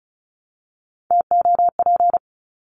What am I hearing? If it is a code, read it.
Morse code, heard as a keyed tone, about 34 wpm, 700 Hz. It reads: TOP
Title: Top 500 Words Repeated 35wpm